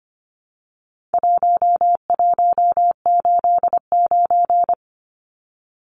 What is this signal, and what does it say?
Morse code, heard as a keyed tone, about 25 wpm, 700 Hz. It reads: 1189
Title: Numbers - 4-Digit 25wpm